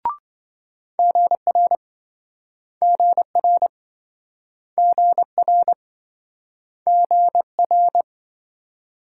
Morse code, heard as a keyed tone, about 28 wpm, 700 Hz. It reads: GR GR GR GR